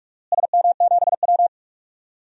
Morse code, heard as a keyed tone, about 46 wpm, 700 Hz. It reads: SM7W
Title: DXCC-100 Call Signs - 4 Characters - Repeated 3x - 45wpm